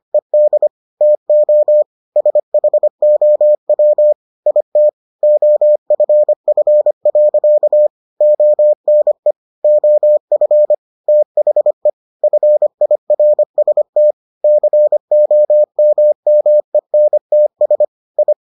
Morse code, heard as a keyed tone, about 25 wpm, 600 Hz. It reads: ED TO SHOW IT OFF. ONE OF THE FIRST COMMENTS I